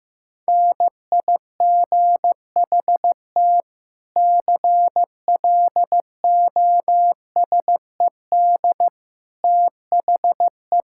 Morse code, heard as a keyed tone, about 15 wpm, 700 Hz. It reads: NIGHT CLOSED THE
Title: Sentences from Top 1600 Words 15wpm